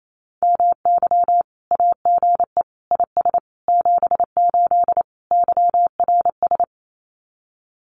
Morse code, heard as a keyed tone, about 28 wpm, 700 Hz. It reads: MY AGE IS 78 YRS
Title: Common QSO Phrases Encore 28wpm